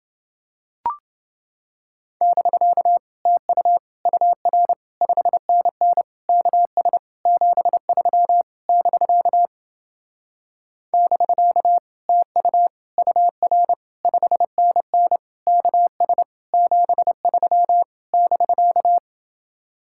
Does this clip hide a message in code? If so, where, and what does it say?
Yes: Morse code from 2.2 to 19.0 s, heard as a keyed tone, about 30 wpm, 700 Hz. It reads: <BK> TU UR 5NN KS 73 <BK> <BK> TU UR 5NN KS 73 <BK>